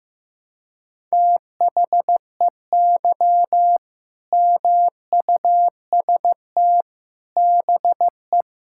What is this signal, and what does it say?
Morse code, heard as a keyed tone, about 15 wpm, 700 Hz. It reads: THEY MUST BE
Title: Sentences from Top 1000 Words - Review Part 5 - 15wpm